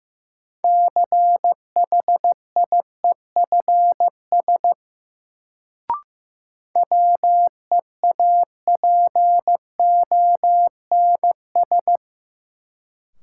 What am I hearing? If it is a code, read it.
Morse code, heard as a keyed tone, about 16 wpm, 700 Hz. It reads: CHIEFS WEAPONS
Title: Sets of 5 Words Encore - New Word Drill - 15wpm